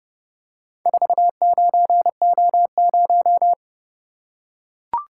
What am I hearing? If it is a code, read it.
Morse code, heard as a keyed tone, about 29 wpm, 700 Hz. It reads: 49O0